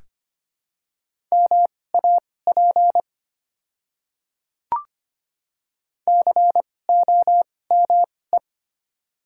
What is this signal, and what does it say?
Morse code, heard as a keyed tone, about 25 wpm, spaced as for 18 wpm, 700 Hz. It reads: MAP COME
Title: 2x Character Spacing - Top 500 Words - 25wpm